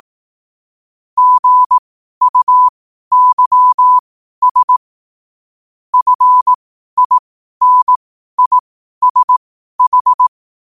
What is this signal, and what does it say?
Morse code, heard as a keyed tone, about 18 wpm, spaced as for 12 wpm, 1000 Hz. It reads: GUYS FINISH
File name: qotdfiles / 12wpm